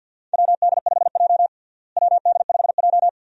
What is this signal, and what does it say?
Morse code, heard as a keyed tone, about 52 wpm, 700 Hz. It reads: WD5J WD5J
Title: USA Call Signs - 4 Characters - Repeated 2x - 50wpm